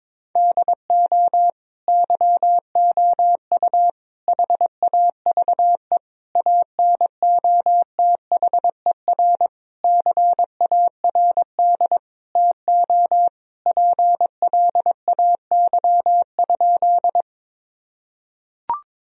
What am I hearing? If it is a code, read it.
Morse code, heard as a keyed tone, about 22 wpm, 700 Hz. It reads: DO YOU HAVE ANOTHER CARD TO PLAY?
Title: Sentences from Top 1000 Words - Review Part 4 - 22wpm